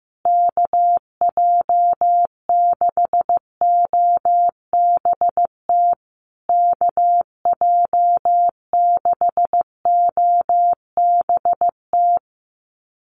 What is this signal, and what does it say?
Morse code, heard as a keyed tone, about 15 wpm, 700 Hz. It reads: KJ6OBT KJ6OBT
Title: USA Call Signs Repeated 3x 15wpm